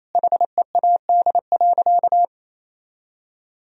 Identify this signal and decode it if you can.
Morse code, heard as a keyed tone, about 28 wpm, 700 Hz. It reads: HEAD.
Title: Sentences from Top 300 Words and Abbreviations 28wpm